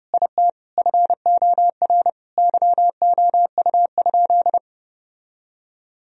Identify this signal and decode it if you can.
Morse code, heard as a keyed tone, about 30 wpm, 700 Hz. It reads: IT FOR YOU?
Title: Sentences from Top 1000 Words - Review Part 2 - 30wpm